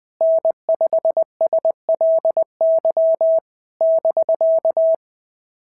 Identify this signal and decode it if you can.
Morse code, heard as a keyed tone, about 20 wpm, 650 Hz. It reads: N5SLY <BK>
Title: POTA QSO Practice - 20wpm